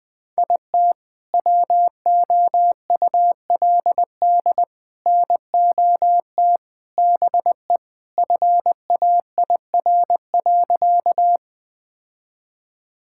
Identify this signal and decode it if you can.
Morse code, heard as a keyed tone, about 21 wpm, 700 Hz. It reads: IT WOULD NOT BE FAIR.
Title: Sentences with 3-Word Phrases - 20wpm